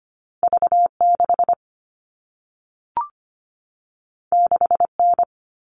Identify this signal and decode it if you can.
Morse code, heard as a keyed tone, about 25 wpm, 700 Hz. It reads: V6 6N
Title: All Cut Numbers - 2-Digit 25wpm